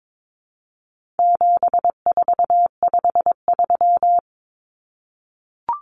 Morse code, heard as a keyed tone, about 22 wpm, 700 Hz. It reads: 7453
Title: Numbers - 4-Digit 22wpm